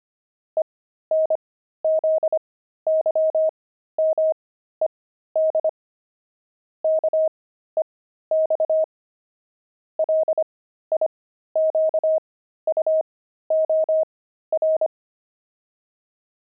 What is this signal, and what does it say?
Morse code, heard as a keyed tone, about 25 wpm, spaced as for 13 wpm, 630 Hz. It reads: ENZYMED KEX LIQUOR
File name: pangram-fs-13-cs-25-fuji